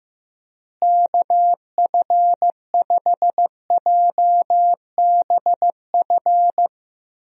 Morse code, heard as a keyed tone, about 16 wpm, 700 Hz. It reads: KF5JBF
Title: USA Call Signs - 6 Characters - Repeated 2x - 15wpm